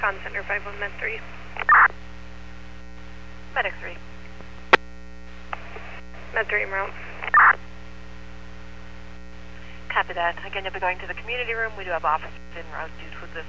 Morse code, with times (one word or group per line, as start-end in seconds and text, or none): none